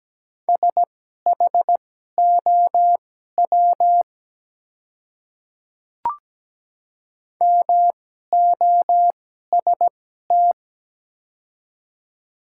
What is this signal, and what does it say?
Morse code, heard as a keyed tone, about 17 wpm, spaced as for 12 wpm, 700 Hz. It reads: SHOW MOST
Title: 2x Character Spacing - Top 300 Words - 17wpm